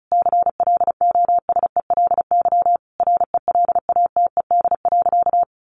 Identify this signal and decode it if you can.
Morse code, heard as a keyed tone, about 34 wpm, 700 Hz. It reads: CLOSELY RELATED.
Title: Sentences from Top 1400 Words 35wpm